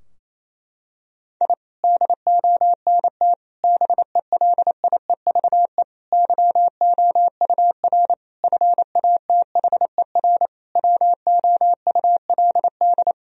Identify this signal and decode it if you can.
Morse code, heard as a keyed tone, about 27 wpm, 700 Hz. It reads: I DONT BELIEVE YOUR FATHER WOULD
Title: Sentences from Top 1000 Words - Review Part 3 - 28wpm